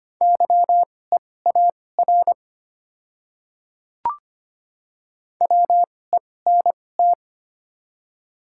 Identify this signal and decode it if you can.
Morse code, heard as a keyed tone, about 25 wpm, spaced as for 18 wpm, 700 Hz. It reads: YEAR WENT
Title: 2x Character Spacing - Top 300 Words - 25wpm